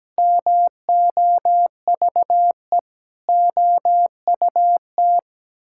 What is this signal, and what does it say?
Morse code, heard as a keyed tone, about 17 wpm, 700 Hz. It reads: MOVE OUT
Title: Sentences from Top 1000 Words - Review Part 6 - 17wpm